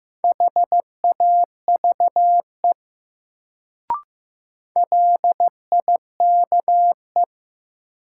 Morse code, heard as a keyed tone, about 15 wpm, 700 Hz. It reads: HAVE LIKE
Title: Sentences from Top 100 Words - New Word Drill - 15wpm